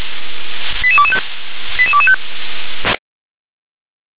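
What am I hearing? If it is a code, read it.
Five-tone selective call (ZVEI), heard as quick runs of five tones, tones of about 70 ms. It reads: 88225 88225
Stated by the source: ZVEI five-tone selective call received by SDR.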